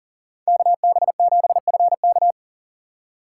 Morse code, heard as a keyed tone, about 40 wpm, 700 Hz. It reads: KB7FK
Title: USA Call Signs - 5 Characters - 2x Character Spacing - 40wpm